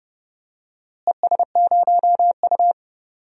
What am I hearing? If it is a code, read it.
Morse code, heard as a keyed tone, about 30 wpm, 700 Hz. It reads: ES0U
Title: DXCC-100 Call Signs - 4 Characters - Repeated 3x - 30wpm